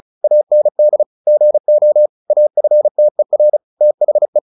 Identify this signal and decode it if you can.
Morse code, heard as a keyed tone, about 34 wpm, 600 Hz. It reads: AND GO AFTER THE
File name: the old man and the sea / the old man and the sea 1 35wpm